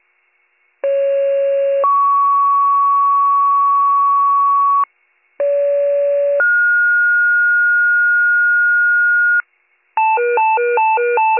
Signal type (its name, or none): two-tone page